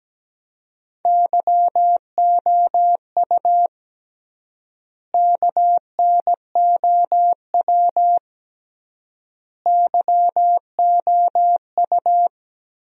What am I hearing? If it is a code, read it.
Morse code, heard as a keyed tone, about 17 wpm, 700 Hz. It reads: YOU KNOW YOU
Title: 3x Word-Spacing - Sentences from Top 500 Words - 17wpm